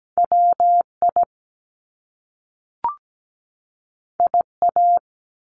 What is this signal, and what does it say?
Morse code, heard as a keyed tone, about 17 wpm, 700 Hz. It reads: WI IA
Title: US States 17wpm